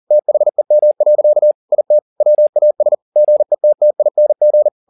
Morse code, heard as a keyed tone, about 40 wpm, 600 Hz. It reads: THEM. IT WAS GETTING
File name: the old man and the sea 2 40wpm